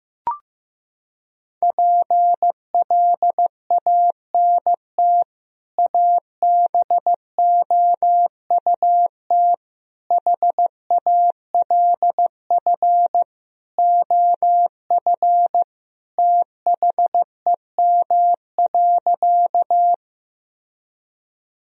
Morse code, heard as a keyed tone, about 15 wpm, 700 Hz. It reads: PLANT ABOUT HALF OF THEM.
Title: Sentences from Top 1000 Words - Review Part 1 - 15wpm